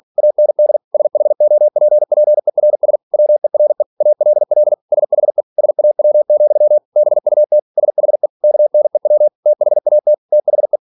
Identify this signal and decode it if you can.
Morse code, heard as a keyed tone, about 46 wpm, 600 Hz. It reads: AND SHOPPERS WERE ALL SHE SAW, BUT SHE KNEW THAT THE
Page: Never 47wpm